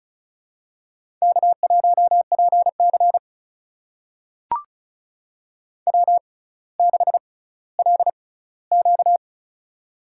Morse code, heard as a keyed tone, about 36 wpm, spaced as for 18 wpm, 700 Hz. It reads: K1PC W6LQ
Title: USA Call Signs - 6x Character Spacing - 35wpm